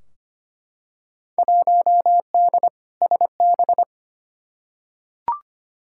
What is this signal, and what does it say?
Morse code, heard as a keyed tone, about 25 wpm, 700 Hz. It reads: 1D SB
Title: Field Day Exchanges - 25wpm